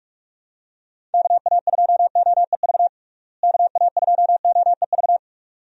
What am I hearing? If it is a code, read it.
Morse code, heard as a keyed tone, about 46 wpm, 700 Hz. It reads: KA2OEV KA2OEV
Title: USA Call Signs - 6 Characters - Repeated 2x - 45wpm